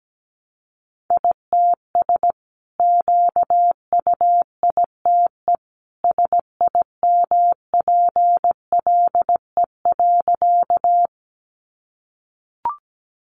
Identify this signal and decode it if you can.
Morse code, heard as a keyed tone, about 17 wpm, 700 Hz. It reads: ITS QUITE SIMPLE.
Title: Sentences from Top 700 Words 17wpm